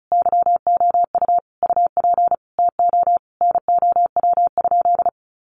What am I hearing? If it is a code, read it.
Morse code, heard as a keyed tone, about 34 wpm, 700 Hz. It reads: YOU UP TO NOW?
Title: Sentences from Top 100 Words 35wpm